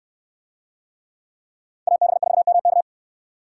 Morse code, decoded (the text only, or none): AB4MK